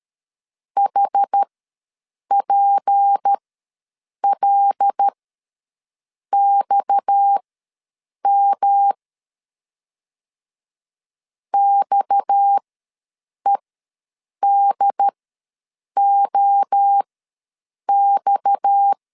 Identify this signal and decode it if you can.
Morse code, heard as a keyed tone, about 13 wpm, spaced as for 7 wpm, 800 Hz. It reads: HPLXM XEDOX
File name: LEZIONI / 6 PKX0 / 6 PKX ws 12wpm 3106